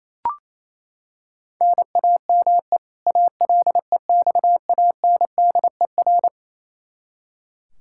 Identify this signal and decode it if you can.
Morse code, heard as a keyed tone, about 27 wpm, 700 Hz. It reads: NAME ALEXANDER
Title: Practice QSO - Exchange 1 - Beginning - 28wpm